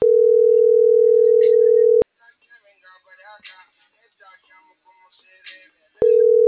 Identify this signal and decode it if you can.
Ringback tone (North America), 440 + 480 Hz.